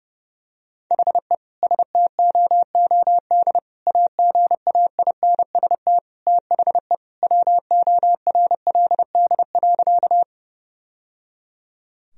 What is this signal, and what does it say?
Morse code, heard as a keyed tone, about 30 wpm, 700 Hz. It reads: HE STOOD AGAINST THE WORLD.
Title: Sentences from Top 1000 Words - Review Part 2 - 30wpm